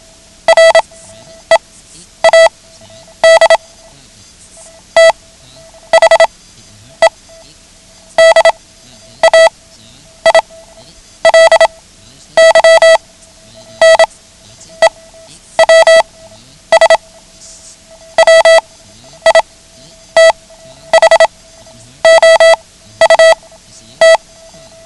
Morse code, heard as a keyed tone, about 27 wpm, spaced as for 10 wpm, 660 Hz. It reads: READ THE DAILY NEWS WITHOUT
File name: ham radio / CodeQuick / Code Quick 09